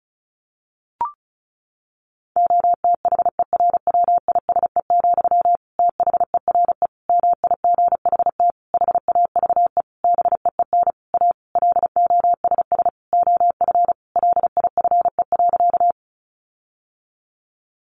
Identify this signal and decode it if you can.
Morse code, heard as a keyed tone, about 34 wpm, 700 Hz. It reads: OTHERWISE, THERE MIGHT HAVE BEEN A LOSS OF LIFE.
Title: Sentences from Top 1400 Words 35wpm